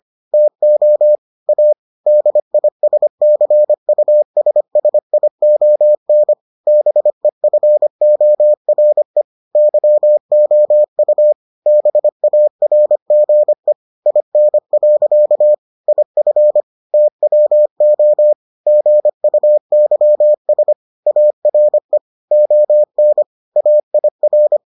Morse code, heard as a keyed tone, about 25 wpm, 600 Hz. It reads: TO A DISCUSSION BEFORE YOU BARGE IN. IF TWO GUYS ARE ON AIR